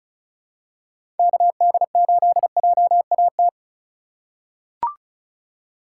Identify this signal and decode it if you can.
Morse code, heard as a keyed tone, about 34 wpm, 700 Hz. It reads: KD8JAT
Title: USA Call Signs - 6 Characters - 2x Character Spacing - 35wpm